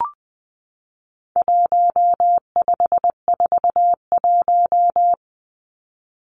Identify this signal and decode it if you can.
Morse code, heard as a keyed tone, about 20 wpm, 700 Hz. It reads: 1541
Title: Numbers - 4-Digit 20wpm